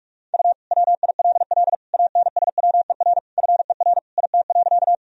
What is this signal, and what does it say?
Morse code, heard as a keyed tone, about 46 wpm, 700 Hz. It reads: U WILL ANSWER FER IT.
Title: Sentences from Top 300 Words and Abbreviations 45wpm